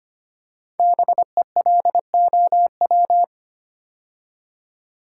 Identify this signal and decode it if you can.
Morse code, heard as a keyed tone, about 25 wpm, 700 Hz. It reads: BELOW
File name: 5 Letter Words - First 3 Letters Given - 25wpm